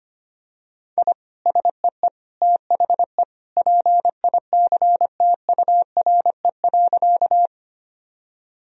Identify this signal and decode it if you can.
Morse code, heard as a keyed tone, about 25 wpm, 700 Hz. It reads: I SEE THE PICTURE.
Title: Sentences from Top 1000 Words - Review Part 1 - 25wpm